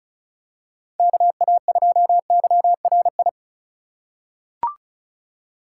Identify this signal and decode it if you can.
Morse code, heard as a keyed tone, about 34 wpm, 700 Hz. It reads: KA2YRI
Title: USA Call Signs - 2x Character Spacing - 35wpm